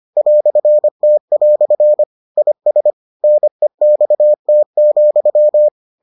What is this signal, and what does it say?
Morse code, heard as a keyed tone, about 25 wpm, 600 Hz. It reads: "T" IS NEXT,